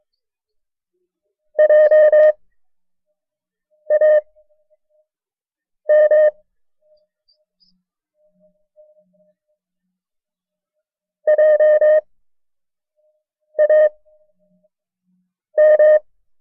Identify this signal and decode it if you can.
Morse code, heard as a keyed tone, about 23 wpm, spaced as for 5 wpm, 610 Hz. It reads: JAM JAM
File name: FreeFiles / CW-practice / KB6MT-tape1-side2